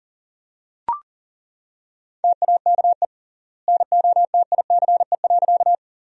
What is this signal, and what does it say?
Morse code, heard as a keyed tone, about 40 wpm, 700 Hz. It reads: TAKE NOTICE.
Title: 3x Word-Spacing - Sentences from Top 400 Words - 40wpm